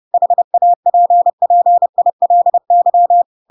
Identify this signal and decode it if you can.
Morse code, heard as a keyed tone, about 30 wpm, 700 Hz. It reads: HAPPILY
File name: Pinocchio / Pinocchio ch3 30 30 ik1hge